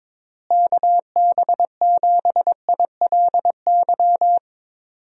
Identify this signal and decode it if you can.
Morse code, heard as a keyed tone, about 22 wpm, 700 Hz. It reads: KB7ILY
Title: USA Call Signs - 6 Characters - Repeated 3x - 22wpm